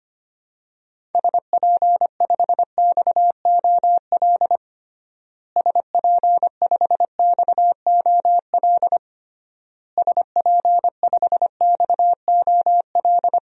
Morse code, heard as a keyed tone, about 25 wpm, 700 Hz. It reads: SP5XOL SP5XOL SP5XOL
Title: DXCC-100 Call Signs - 6 Characters - Repeated 3x - 25wpm